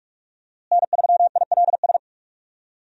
Morse code, heard as a keyed tone, about 42 wpm, 700 Hz. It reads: N3ILS
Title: USA Call Signs - 5 Characters - Repeated 2x - 45wpm